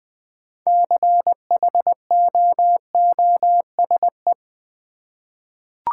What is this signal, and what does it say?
Morse code, heard as a keyed tone, about 20 wpm, 700 Hz. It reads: CHOOSE
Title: 6 Letter Words - First 3 Letters Given - 20wpm